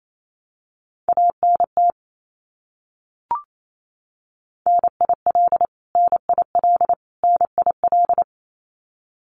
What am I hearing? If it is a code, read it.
Morse code, heard as a keyed tone, about 28 wpm, 700 Hz. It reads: ANT NIL NIL NIL
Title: Abbreviations Repeated 3x - 28wpm